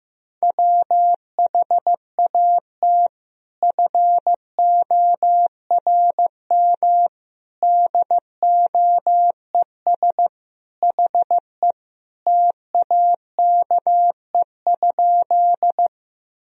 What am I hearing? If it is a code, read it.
Morse code, heard as a keyed tone, about 16 wpm, 700 Hz. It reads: WHAT FORM DOES HE TAKE?
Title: Sentences from Top 1000 Words - Review Part 1 - 15wpm